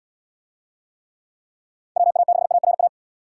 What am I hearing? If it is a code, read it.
Morse code, heard as a keyed tone, about 68 wpm, 700 Hz. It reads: WA6ARA